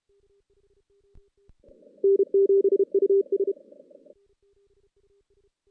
Morse code, heard as a keyed tone, about 31 wpm, 390 Hz. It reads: N7US
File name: 2016 JAN CW NAQP / 2016 Jan naqp k5ot - 0-5